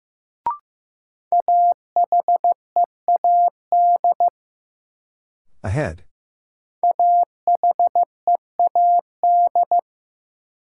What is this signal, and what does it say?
Morse code, heard as a keyed tone, about 15 wpm, 700 Hz. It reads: AHEAD AHEAD
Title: Sentences from Top 1000 Words - Review Part 6 - New Word Drill - 15wpm